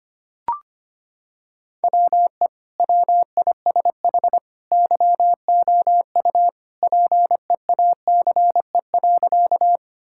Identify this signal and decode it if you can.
Morse code, heard as a keyed tone, about 25 wpm, 700 Hz. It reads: WE WISH YOU PEACE.
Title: Sentences from Top 1600 Words 25wpm